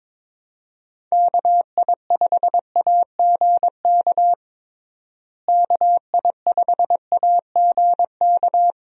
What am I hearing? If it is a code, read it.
Morse code, heard as a keyed tone, about 22 wpm, 700 Hz. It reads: KI5AGK KI5AGK
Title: USA Call Signs - 6 Characters - Repeated 2x - 22wpm